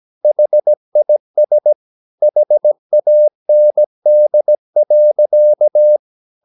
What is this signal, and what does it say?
Morse code, heard as a keyed tone, about 17 wpm, 600 Hz. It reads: HIS HAND.
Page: Romanen Never af Ken Follet på engelsk.